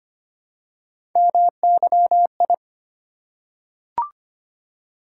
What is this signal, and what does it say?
Morse code, heard as a keyed tone, about 25 wpm, 700 Hz. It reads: MYI
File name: Three Letter-Number Combinations - 25wpm